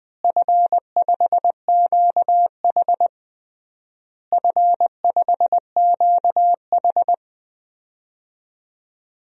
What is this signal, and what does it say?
Morse code, heard as a keyed tone, about 20 wpm, 700 Hz. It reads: F5QH F5QH
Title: DXCC-100 Call Signs - 4 Characters - Repeated 3x - 20wpm